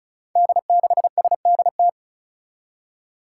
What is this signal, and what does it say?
Morse code, heard as a keyed tone, about 34 wpm, 700 Hz. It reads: D6SDT
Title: Five Challenging Letter-Number Combinations 35wpm